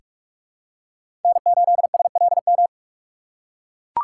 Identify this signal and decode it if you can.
Morse code, heard as a keyed tone, about 43 wpm, 700 Hz. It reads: N8SLM
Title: USA Call Signs - 5 Characters - 6x Character Spacing - 45wpm